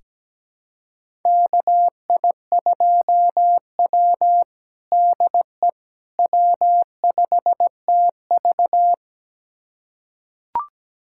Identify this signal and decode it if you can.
Morse code, heard as a keyed tone, about 17 wpm, 700 Hz. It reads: KI2W DE W5TV
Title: USA Callsign DE Callsign - 4 Characters - 17wpm